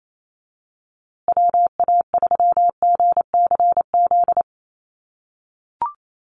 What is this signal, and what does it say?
Morse code, heard as a keyed tone, about 28 wpm, 700 Hz. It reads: WA3GCZ
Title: USA Call Signs Repeated 3x 28wpm